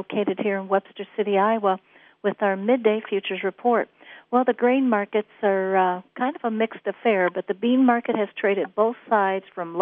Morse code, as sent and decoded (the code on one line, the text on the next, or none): none